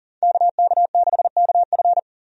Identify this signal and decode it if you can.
Morse code, heard as a keyed tone, about 40 wpm, 700 Hz. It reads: KK6KF
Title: USA Call Signs - 5 Characters - 40wpm